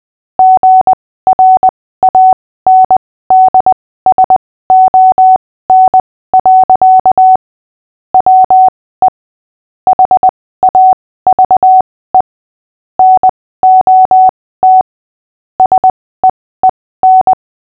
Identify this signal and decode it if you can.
Morse code, heard as a keyed tone, about 20 wpm, spaced as for 15 wpm, 750 Hz. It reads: GRANDSON. WE HAVE NOT SEEN
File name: cw / fists / Short-Story-5-15wpm